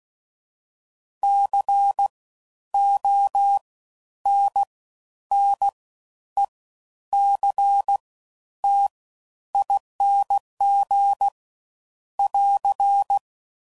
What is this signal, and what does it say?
Morse code, heard as a keyed tone, about 16 wpm, spaced as for 10 wpm, 800 Hz. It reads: CONNECTING <AR>
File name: Suffixes / Special INGs / Fast Ings